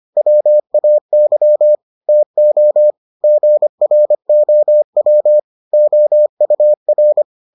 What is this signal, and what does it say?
Morse code, heard as a keyed tone, about 25 wpm, 600 Hz. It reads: WAY TO GROW OUR